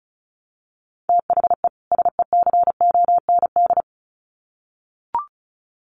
Morse code, heard as a keyed tone, about 34 wpm, 700 Hz. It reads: THE SECOND
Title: Sets of 2 Words 35wpm